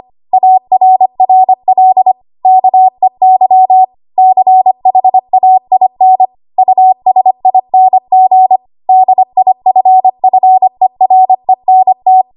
Morse code, heard as a keyed tone, about 25 wpm, 750 Hz. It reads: ARRL KEY CHAIN USING DIFFERENT
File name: Archive / 25 WPM / 25WPM